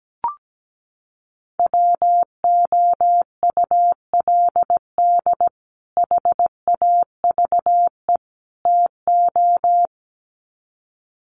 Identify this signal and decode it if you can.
Morse code, heard as a keyed tone, about 17 wpm, 700 Hz. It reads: WOULD HAVE TO
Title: Sets of 3 Words 17wpm